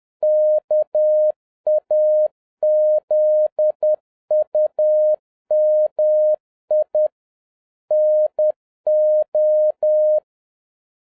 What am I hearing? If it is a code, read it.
Morse code, heard as a keyed tone, about 10 wpm, 610 Hz. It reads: KAZUMI NO